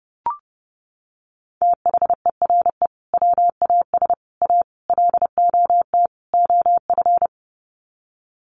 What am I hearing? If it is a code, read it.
Morse code, heard as a keyed tone, about 30 wpm, 700 Hz. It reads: THERE WAS A LOT OF
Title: Sets of 5 Words 30wpm